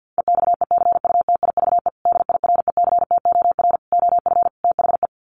Morse code, heard as a keyed tone, about 50 wpm, 700 Hz. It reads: EXECUTIVE DIRECTOR OF THE
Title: Sets of 5 Words Encore2 50wpm